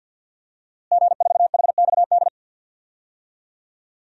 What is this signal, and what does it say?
Morse code, heard as a keyed tone, about 52 wpm, 700 Hz. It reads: G4HXD
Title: DXCC-100 Call Signs - 5 Characters - Repeated 2x - 50wpm